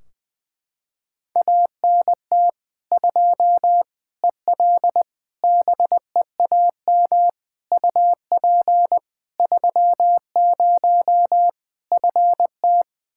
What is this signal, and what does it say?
Morse code, heard as a keyed tone, about 20 wpm, 700 Hz. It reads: ANT 2 EL BEAM UP 30 FT
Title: Practice QSO - Exchange 2 - Info - 20wpm